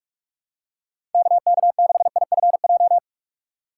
Morse code, heard as a keyed tone, about 46 wpm, 700 Hz. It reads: KK6IFJ
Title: USA Call Signs - 6x Character Spacing - 45wpm